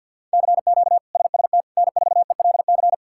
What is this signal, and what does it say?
Morse code, heard as a keyed tone, about 51 wpm, 700 Hz. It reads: CQ SST N4ELC
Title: K1USN SST QSO Practice - 50wpm